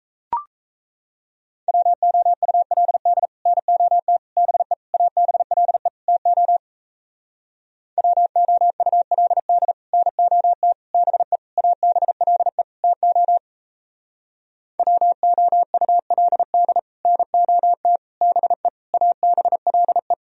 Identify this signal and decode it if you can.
Morse code, heard as a keyed tone, about 38 wpm, 700 Hz. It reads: WOULD NOT BE ABLE TO WOULD NOT BE ABLE TO WOULD NOT BE ABLE